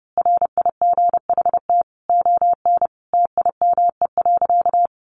Morse code, heard as a keyed tone, about 30 wpm, 700 Hz. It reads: RIGHT ON TIME.
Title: A Sound Of Thunder 30wpm